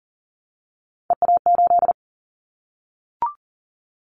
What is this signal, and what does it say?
Morse code, heard as a keyed tone, about 40 wpm, 700 Hz. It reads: EA8
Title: Callsign Prefixes for Top 50 DXCC - 40wpm